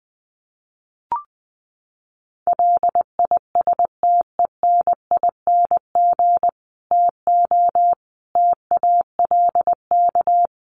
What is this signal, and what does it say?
Morse code, heard as a keyed tone, about 20 wpm, 700 Hz. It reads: LISTENING TO TALK
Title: Sets of 5 Words 20wpm